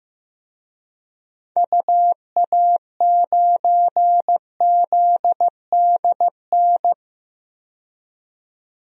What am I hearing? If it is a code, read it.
Morse code, heard as a keyed tone, about 15 wpm, 700 Hz. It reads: UA9ZDN
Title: DXCC-100 Call Signs - 6 Characters - 15wpm